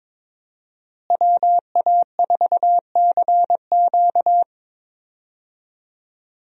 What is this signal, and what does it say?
Morse code, heard as a keyed tone, about 22 wpm, 700 Hz. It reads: WA4CQ
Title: USA Call Signs - 5 Characters - Repeated 2x - 22wpm